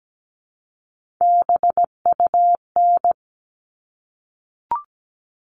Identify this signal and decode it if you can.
Morse code, heard as a keyed tone, about 17 wpm, 700 Hz. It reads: BUN
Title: All Cut Numbers - 3-Digit 17wpm